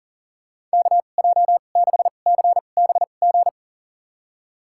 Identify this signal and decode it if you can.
Morse code, heard as a keyed tone, about 38 wpm, spaced as for 28 wpm, 690 Hz. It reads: KJ6CBG
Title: USA Call Signs - 2x Character Spacing - 40wpm